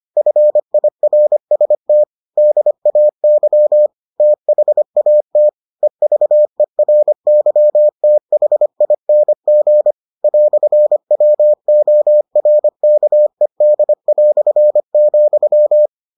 Morse code, heard as a keyed tone, about 25 wpm, 600 Hz. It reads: FIRST DAY THAT EVERYTHING "WORKED",